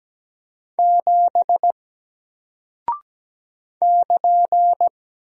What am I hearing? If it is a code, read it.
Morse code, heard as a keyed tone, about 17 wpm, 700 Hz. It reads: 7 <KN>
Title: Single Character (Letter:Number:Punctuation:Prosign) Alt 17wpm